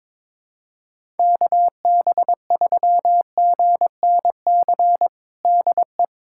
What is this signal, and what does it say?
Morse code, heard as a keyed tone, about 21 wpm, 700 Hz. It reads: KB3GNC DE
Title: USA Callsign DE Callsign - 6 Characters - 22wpm